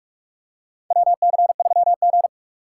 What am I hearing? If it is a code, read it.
Morse code, heard as a keyed tone, about 46 wpm, 700 Hz. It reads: WC3G
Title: USA Call Signs - 4 Characters - 6x Character Spacing - 45wpm